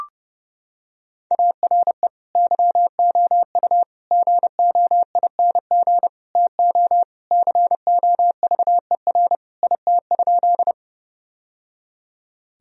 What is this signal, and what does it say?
Morse code, heard as a keyed tone, about 30 wpm, 700 Hz. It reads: ARE YOU GOING TO COVER IT?
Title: Sentences from Top 1000 Words - Review Part 1 - 30wpm